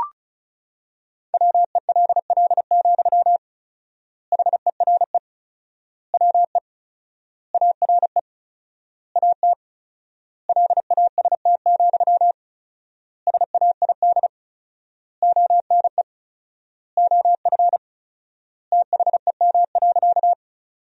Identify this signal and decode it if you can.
Morse code, heard as a keyed tone, about 34 wpm, 700 Hz. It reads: WELL, HERE WE ARE AT LAST, SAID ONE OF THEM.
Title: 4x Word-Spacing - Sentences from Top 300 Words - 35wpm